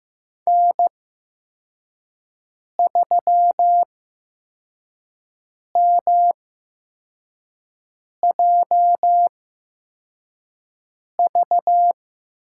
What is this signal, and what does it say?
Morse code, heard as a keyed tone, about 15 wpm, spaced as for 4 wpm, 700 Hz. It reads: N3MJV
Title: USA Call Signs - 5 Characters - 8x Character Spacing - 15wpm